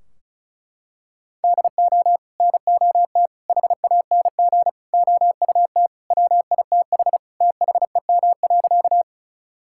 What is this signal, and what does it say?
Morse code, heard as a keyed tone, about 34 wpm, 700 Hz. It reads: DO NOT HANG OUT WITH THEM.
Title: Sentences from Top 1600 Words 35wpm